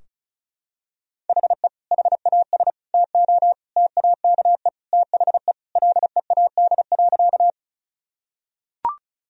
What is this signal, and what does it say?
Morse code, heard as a keyed tone, about 36 wpm, 700 Hz. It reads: HE HAS TO TAKE THE LEAD.
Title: Sentences from Top 1000 Words - Review Part 2 - 35wpm